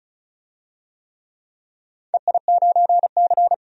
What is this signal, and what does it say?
Morse code, heard as a keyed tone, about 36 wpm, 700 Hz. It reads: EI9C
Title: DXCC-100 Call Signs - 4 Characters - 35wpm